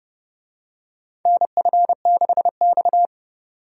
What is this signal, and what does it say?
Morse code, heard as a keyed tone, about 31 wpm, 700 Hz. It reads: NF6X
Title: USA Call Signs - 4 Characters - 30wpm